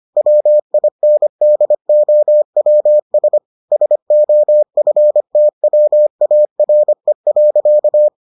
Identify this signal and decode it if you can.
Morse code, heard as a keyed tone, about 25 wpm, 600 Hz. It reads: WINDOWS SOFTWARE.